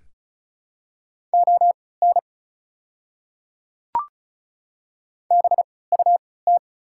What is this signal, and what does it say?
Morse code, heard as a keyed tone, about 36 wpm, spaced as for 20 wpm, 700 Hz. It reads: ON BUT
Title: 3x Character Spacing - Top 300 Words - 35wpm